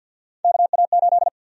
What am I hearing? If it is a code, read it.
Morse code, heard as a keyed tone, about 52 wpm, 700 Hz. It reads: KA8
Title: USA Call Sign Prefixes - 50wpm